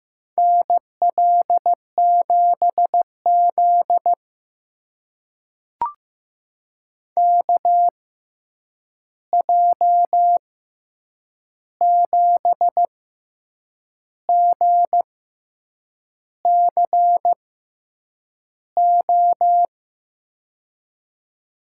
Morse code, heard as a keyed tone, about 15 wpm, spaced as for 7 wpm, 700 Hz. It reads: NL7Z KJ7GCO
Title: USA Call Signs - 6x Character Spacing - 15wpm